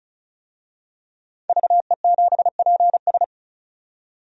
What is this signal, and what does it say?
Morse code, heard as a keyed tone, about 36 wpm, 700 Hz. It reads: VE7PS
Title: DXCC-100 Call Signs - 5 Characters - 35wpm